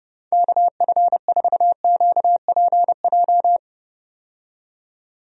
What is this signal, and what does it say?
Morse code, heard as a keyed tone, about 32 wpm, 700 Hz. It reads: KF4QPJ
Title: USA Call Signs - 6 Characters - 30wpm